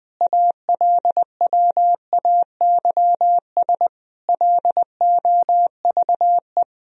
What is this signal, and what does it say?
Morse code, heard as a keyed tone, about 20 wpm, 700 Hz. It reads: ALWAYS LOVE
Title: Sentences from Top 400 Words and Abbreviations 20wpm